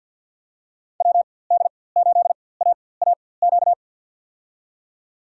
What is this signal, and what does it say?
Morse code, heard as a keyed tone, about 52 wpm, spaced as for 23 wpm, 700 Hz. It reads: WD8AAQ
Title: USA Call Signs - 6 Characters - 4x Character Spacing - 50wpm